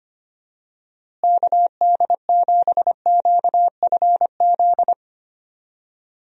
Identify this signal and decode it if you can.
Morse code, heard as a keyed tone, about 25 wpm, 700 Hz. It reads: KD7QFZ
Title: USA Call Signs - 6 Characters - Repeated 3x - 25wpm